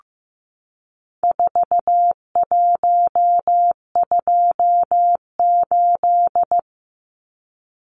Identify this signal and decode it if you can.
Morse code, heard as a keyed tone, about 15 wpm, 700 Hz. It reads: 4128
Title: Numbers - 4-Digit 15wpm